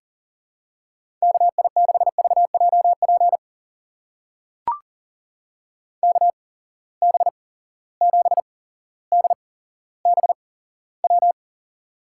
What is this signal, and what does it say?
Morse code, heard as a keyed tone, about 41 wpm, spaced as for 17 wpm, 700 Hz. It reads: KI6VJP KB7DBW
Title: USA Call Signs - 6 Characters - 8x Character Spacing - 40wpm